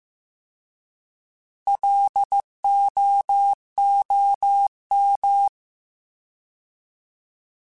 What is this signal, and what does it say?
Morse code, heard as a keyed tone, about 15 wpm, 800 Hz. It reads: LOOM
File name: Alphabetical / Alphabetical mp3s / Ls @13